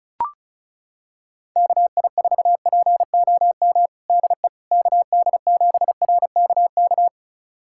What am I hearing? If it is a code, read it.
Morse code, heard as a keyed tone, about 34 wpm, 700 Hz. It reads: KI4POM DE KD7RKK
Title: USA Callsign DE Callsign - 6 Characters - 35wpm